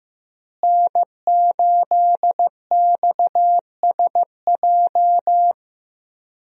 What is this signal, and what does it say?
Morse code, heard as a keyed tone, about 16 wpm, 700 Hz. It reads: N8XSJ
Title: USA Call Signs - 5 Characters - Repeated 3x - 15wpm